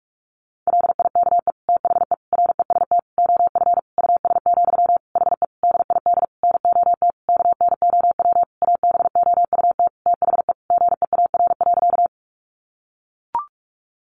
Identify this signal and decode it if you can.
Morse code, heard as a keyed tone, about 44 wpm, 700 Hz. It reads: LIKE THE REST OF US, HE DID NOT KNOW ABOUT THE GEAR.
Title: Sentences from Top 1400 Words 45wpm